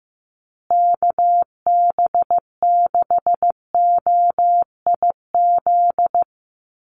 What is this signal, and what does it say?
Morse code, heard as a keyed tone, about 15 wpm, 700 Hz. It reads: KB6OIZ
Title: USA Call Signs Repeated 2x 15wpm